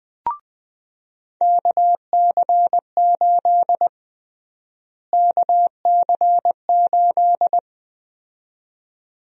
Morse code, heard as a keyed tone, about 20 wpm, 700 Hz. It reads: KC8 KC8
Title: USA Call Sign Prefixes - Repeated 2x - 20wpm